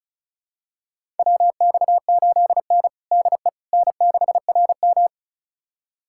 Morse code, heard as a keyed tone, about 34 wpm, 700 Hz. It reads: WX8N DE N6RM
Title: USA Callsign DE Callsign - 4 Characters - 35wpm